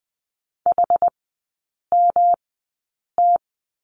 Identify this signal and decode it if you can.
Morse code, heard as a keyed tone, about 20 wpm, spaced as for 8 wpm, 700 Hz. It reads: HMT
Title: 107 Lesson 18e - Letter m - Three Characters 20wpm